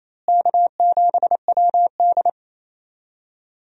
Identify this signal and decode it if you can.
Morse code, heard as a keyed tone, about 28 wpm, 700 Hz. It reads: K7WD
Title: USA Call Signs - 4 Characters - Repeated 3x - 28wpm